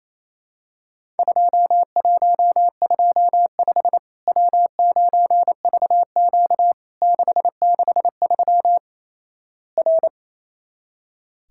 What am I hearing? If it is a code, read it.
Morse code, heard as a keyed tone, about 28 wpm, 700 Hz. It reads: 2125 W9VQ 663 R
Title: Worked All Europe DX Contest - QTC Exchange - 28wpm